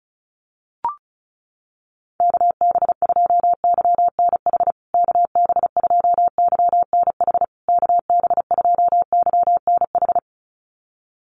Morse code, heard as a keyed tone, about 34 wpm, 700 Hz. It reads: KB2YNH KB2YNH KB2YNH
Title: USA Call Signs Repeated 3x 35wpm